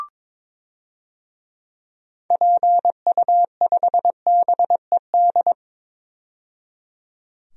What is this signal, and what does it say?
Morse code, heard as a keyed tone, about 21 wpm, 700 Hz. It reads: PU5BED
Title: DXCC-100 Call Signs - 6 Characters - 22wpm